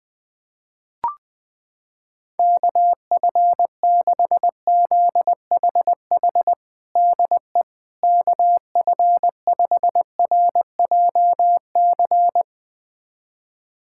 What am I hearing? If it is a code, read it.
Morse code, heard as a keyed tone, about 21 wpm, 700 Hz. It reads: KF6ZHH DE KF5RJC
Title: USA Callsign DE Callsign 20wpm